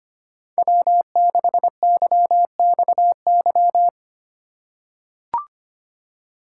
Morse code, heard as a keyed tone, about 25 wpm, 700 Hz. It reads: W6YXY
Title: USA Call Signs - 2x Character Spacing - 25wpm